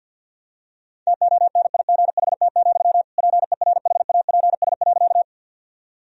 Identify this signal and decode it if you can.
Morse code, heard as a keyed tone, about 52 wpm, 700 Hz. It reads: TONIGHT, PERHAPS.
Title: Sentences from Top 1000 Words - Review Part 3 - 50wpm